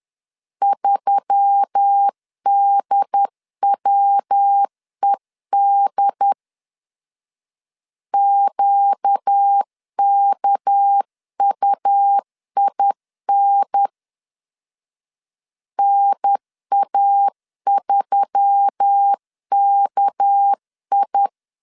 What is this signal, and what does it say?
Morse code, heard as a keyed tone, about 10 wpm, 800 Hz. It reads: 3DWED QKUIN NA3KI
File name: CW / DET / 1 6 1500C S C 10wpm 4252